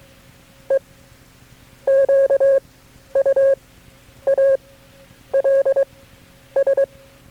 Morse code, heard as a keyed tone, about 23 wpm, spaced as for 9 wpm, 540 Hz. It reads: EQUALS
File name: KB6MT-tape3-side1